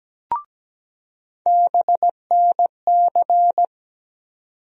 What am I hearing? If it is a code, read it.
Morse code, heard as a keyed tone, about 17 wpm, 700 Hz. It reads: BNC
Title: USA Call Sign Postfixes - 3 Letters 17wpm